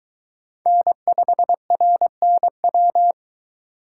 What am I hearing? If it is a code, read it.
Morse code, heard as a keyed tone, about 22 wpm, 700 Hz. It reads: N5RNW